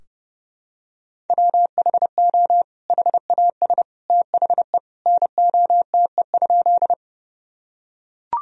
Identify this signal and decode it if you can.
Morse code, heard as a keyed tone, about 30 wpm, 700 Hz. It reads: WHO HAS THE NOTE?
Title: Sentences from Top 1000 Words - Review Part 2 - 30wpm